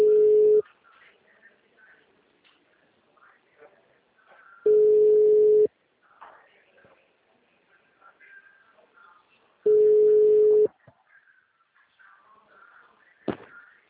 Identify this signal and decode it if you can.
Ringback tone (Europe, 425 Hz).